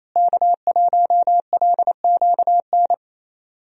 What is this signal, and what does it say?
Morse code, heard as a keyed tone, about 28 wpm, 700 Hz. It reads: K1LQN
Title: USA Call Signs - 5 Characters - Repeated 2x - 28wpm